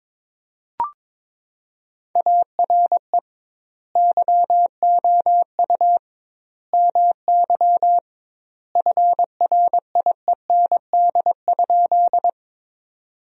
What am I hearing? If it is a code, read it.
Morse code, heard as a keyed tone, about 21 wpm, 700 Hz. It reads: ARE YOU MY FRIEND?
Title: 2x Word-Spacing - Sentences from Top 300 Words - 22wpm